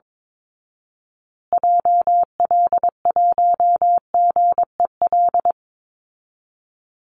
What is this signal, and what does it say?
Morse code, heard as a keyed tone, about 22 wpm, 700 Hz. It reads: JL1GEL
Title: Call Signs Repeated 3x - 22wpm